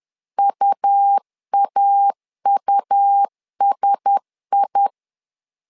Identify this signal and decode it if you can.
Morse code, heard as a keyed tone, about 10 wpm, 800 Hz. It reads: UAUSI
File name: LEZIONI / 5 LBVC / 5 L 10wpm 2259